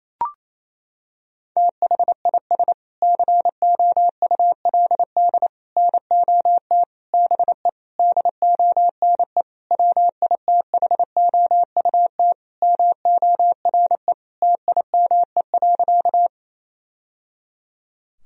Morse code, heard as a keyed tone, about 27 wpm, 700 Hz. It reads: THIS COULD NOT BE DONE WITHOUT MORE TIME.
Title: Sentences from Top 1000 Words - Review Part 3 - 28wpm